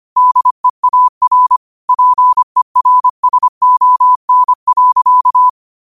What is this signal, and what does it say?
Morse code, heard as a keyed tone, about 25 wpm, 1000 Hz. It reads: DEAR PERSON.
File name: qotdfiles / 25wpm